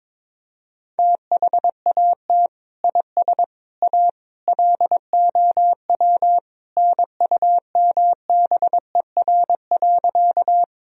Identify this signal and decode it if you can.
Morse code, heard as a keyed tone, about 23 wpm, 700 Hz. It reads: THAT IS A LOW NUMBER.
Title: Sentences from Top 1000 Words - Review Part 1 - 22wpm